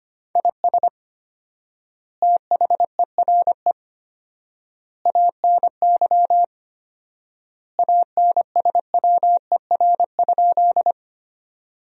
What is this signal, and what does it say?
Morse code, heard as a keyed tone, about 25 wpm, 700 Hz. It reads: IS THERE ANY ANSWER?
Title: 4x Word-Spacing - Sentences from Top 300 Words - 25wpm